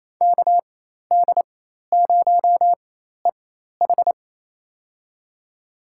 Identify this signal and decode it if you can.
Morse code, heard as a keyed tone, about 27 wpm, spaced as for 13 wpm, 700 Hz. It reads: KD0EH
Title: USA Call Signs - 4x Character Spacing - 28wpm